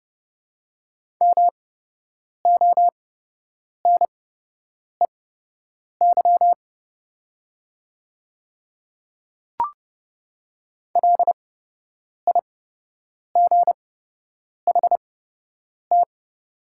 Morse code, heard as a keyed tone, about 30 wpm, spaced as for 8 wpm, 700 Hz. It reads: MONEY LIGHT
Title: 8x Character Spacing - Top 500 Words - 30wpm